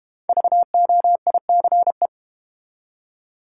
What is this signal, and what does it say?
Morse code, heard as a keyed tone, about 31 wpm, 700 Hz. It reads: VOICE
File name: Easy Words - Speed-Racing with Hints - 25wpm